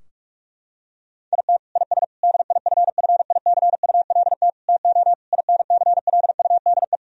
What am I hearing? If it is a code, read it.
Morse code, heard as a keyed tone, about 46 wpm, 700 Hz. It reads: IT IS DIFFICULT TO INCLUDE
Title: Sentences from Top 1000 Words - Review Part 3 - 45wpm